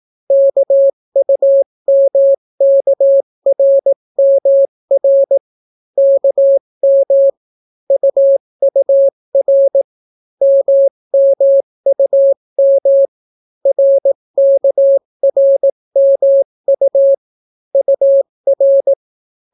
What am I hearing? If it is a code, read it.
Morse code, heard as a keyed tone, about 18 wpm, 550 Hz. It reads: KUMKRMR KM UUR MMUM RKRMU UR